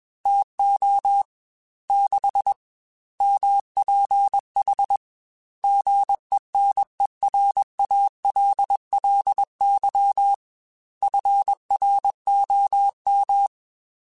Morse code, heard as a keyed tone, about 21 wpm, 800 Hz. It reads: TO 6 MPH GENERALLY FROM
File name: Weather Sets / Weather Sets-mp3s / Weather 4 10 @20